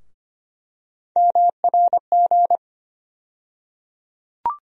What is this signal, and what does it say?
Morse code, heard as a keyed tone, about 25 wpm, 700 Hz. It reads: MRG
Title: USA Call Sign Postfixes - 3 Letters 25wpm